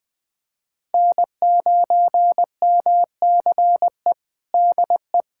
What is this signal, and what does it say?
Morse code, heard as a keyed tone, about 20 wpm, 700 Hz. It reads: N9MCE DE
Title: USA Callsign DE Callsign - 5 Characters - 20wpm